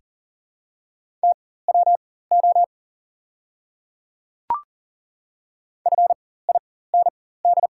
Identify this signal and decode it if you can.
Morse code, heard as a keyed tone, about 39 wpm, spaced as for 18 wpm, 700 Hz. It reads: TWO FIND
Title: 4x Character Spacing - Top 200 Words - 40wpm